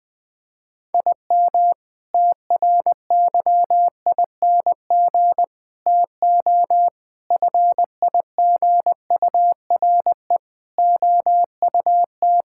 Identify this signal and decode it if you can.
Morse code, heard as a keyed tone, about 20 wpm, 700 Hz. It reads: IM TRYING TO FIGURE OUT
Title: Sentences with 6-Word Phrases - 20wpm